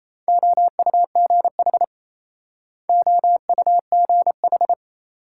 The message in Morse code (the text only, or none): OUGH OUGH